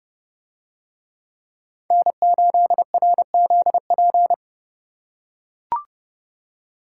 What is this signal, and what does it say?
Morse code, heard as a keyed tone, about 31 wpm, 700 Hz. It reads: N8RZP